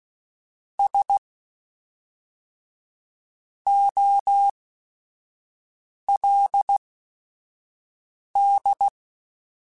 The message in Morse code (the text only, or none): S OLD